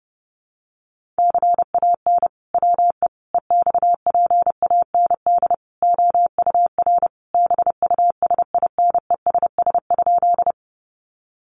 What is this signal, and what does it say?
Morse code, heard as a keyed tone, about 28 wpm, 700 Hz. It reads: CAN WE EXPAND OUR BUSINESS?
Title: Sentences from Top 1400 Words 30wpm